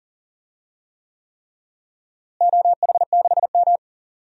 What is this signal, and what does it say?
Morse code, heard as a keyed tone, about 39 wpm, 700 Hz. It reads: OH6M
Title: DXCC-100 Call Signs - 4 Characters - Repeated 3x - 40wpm